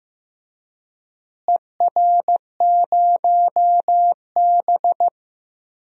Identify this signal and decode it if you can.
Morse code, heard as a keyed tone, about 15 wpm, 700 Hz. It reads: ER0B
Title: DXCC-100 Call Signs - 4 Characters - Repeated 2x - 15wpm